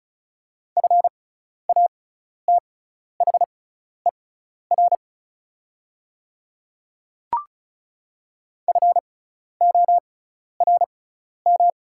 Morse code, heard as a keyed tone, about 36 wpm, spaced as for 12 wpm, 700 Hz. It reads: FATHER FORM
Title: 6x Character Spacing - Top 200 Words - 35wpm